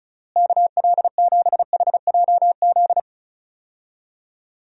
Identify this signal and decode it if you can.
Morse code, heard as a keyed tone, about 34 wpm, 700 Hz. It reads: KL7HJZ
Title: DXCC-100 Call Signs - 6 Characters - Repeated 2x - 35wpm